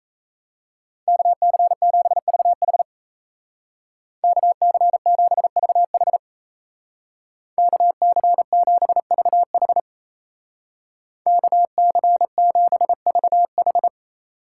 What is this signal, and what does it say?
Morse code, heard as a keyed tone, about 34 wpm, 700 Hz. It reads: KC7VH KC7VH KC7VH KC7VH
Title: Speed Racing - USA 2x2 Call Signs 28wpm